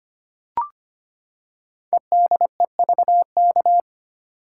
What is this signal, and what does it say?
Morse code, heard as a keyed tone, about 25 wpm, 700 Hz. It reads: EDEVK
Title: Code Groups - 5 Letters - 25wpm